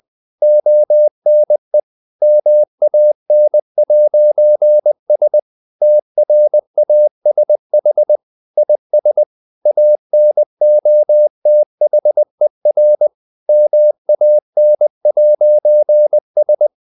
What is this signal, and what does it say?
Morse code, heard as a keyed tone, about 20 wpm, 600 Hz. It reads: ONE MAN'S TRASH IS ANOTHER MAN'S
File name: CW / Practice / EnglishSayings20